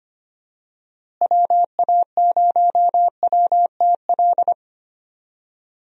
Morse code, heard as a keyed tone, about 25 wpm, 700 Hz. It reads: WA0WTL
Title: USA Call Signs - 6 Characters - Repeated 2x - 25wpm